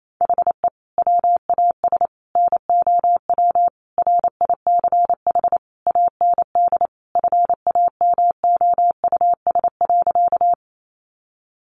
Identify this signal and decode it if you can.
Morse code, heard as a keyed tone, about 28 wpm, 700 Hz. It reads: HE WAS NOW RICH AND FAMOUS.
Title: Sentences from Top 1500 Words 28wpm